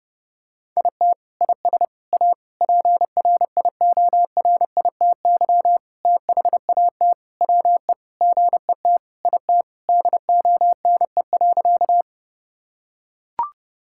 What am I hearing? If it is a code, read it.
Morse code, heard as a keyed tone, about 30 wpm, 700 Hz. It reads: IT IS A PRIORITY THAT WE GET IT DONE.
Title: Sentences from Top 1600 Words 30wpm